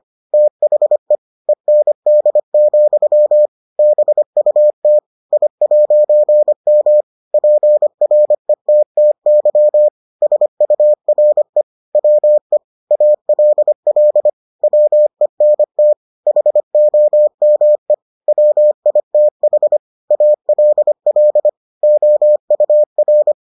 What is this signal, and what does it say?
Morse code, heard as a keyed tone, about 25 wpm, 600 Hz. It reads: THE END, BUT I'M PRETTY SURE WE ALL WENT HOME WITH ALL OUR